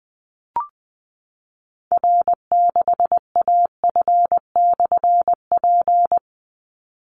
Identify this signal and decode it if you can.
Morse code, heard as a keyed tone, about 20 wpm, 700 Hz. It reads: R6AF/P
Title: Call Signs Repeated 2x - 20wpm